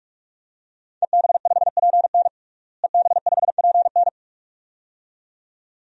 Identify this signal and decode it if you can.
Morse code, heard as a keyed tone, about 42 wpm, 700 Hz. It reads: EB5PN EB5PN
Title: DXCC-100 Call Signs - 5 Characters - Repeated 2x - 45wpm